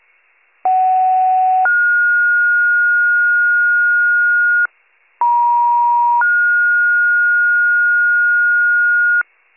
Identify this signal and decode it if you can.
Two-tone page, heard as a one-second tone, then a three-second tone, about 750 then 1430 Hz, about 950 then 1430 Hz, with noise. It is heard from 0.6 to 9.2 s.